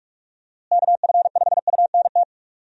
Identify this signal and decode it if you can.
Morse code, heard as a keyed tone, about 43 wpm, 700 Hz. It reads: KF5UNT
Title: USA Call Signs - 6 Characters - Repeated 3x - 45wpm